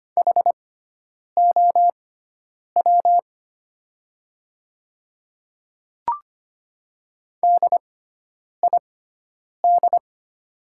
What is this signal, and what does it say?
Morse code, heard as a keyed tone, about 25 wpm, spaced as for 9 wpm, 700 Hz. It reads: HOW DID
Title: 6x Character Spacing - Top 100 Words - 25wpm